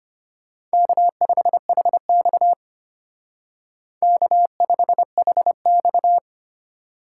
Morse code, heard as a keyed tone, about 32 wpm, 700 Hz. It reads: K5HX K5HX